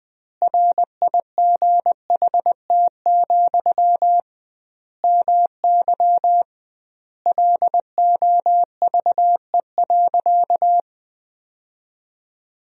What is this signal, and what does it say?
Morse code, heard as a keyed tone, about 21 wpm, 700 Hz. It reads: RIGHT, MY LOVE.
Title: 2x Word-Spacing - Sentences from Top 400 Words - 20wpm